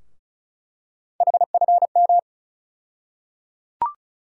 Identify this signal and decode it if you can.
Morse code, heard as a keyed tone, about 36 wpm, 700 Hz. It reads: HFM
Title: USA Call Sign Postfixes - 3 Letters 35wpm